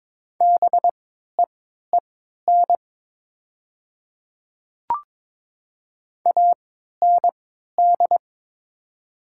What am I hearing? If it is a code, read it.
Morse code, heard as a keyed tone, about 23 wpm, spaced as for 13 wpm, 700 Hz. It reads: BEEN AND
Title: 3x Character Spacing - Top 100 Words - 22wpm